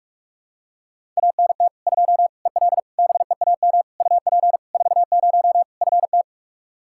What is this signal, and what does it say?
Morse code, heard as a keyed tone, about 42 wpm, 700 Hz. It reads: ANT 2 EL BEAM UP 40 FT
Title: Practice QSO - Exchange 2 - Info - 45wpm